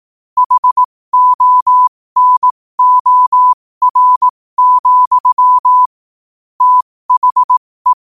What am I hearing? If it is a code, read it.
Morse code, heard as a keyed tone, about 18 wpm, spaced as for 15 wpm, 1000 Hz. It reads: HONOR, THE
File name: qotdfiles / 15wpm